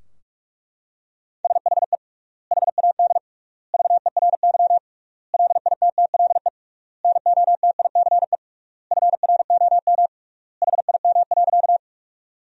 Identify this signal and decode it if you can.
Morse code, heard as a keyed tone, about 41 wpm, 700 Hz. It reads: SHE HAD VERY LITTLE NOTICE FROM HIM.
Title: 3x Word-Spacing - Sentences from Top 400 Words - 45wpm